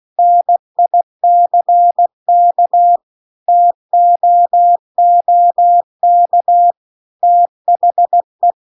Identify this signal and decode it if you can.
Morse code, heard as a keyed tone, about 16 wpm, 700 Hz. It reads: NICK TOOK THE